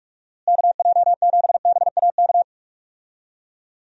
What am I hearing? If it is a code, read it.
Morse code, heard as a keyed tone, about 46 wpm, 700 Hz. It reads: KJ7BAK
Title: USA Call Signs - 6 Characters - Repeated 2x - 45wpm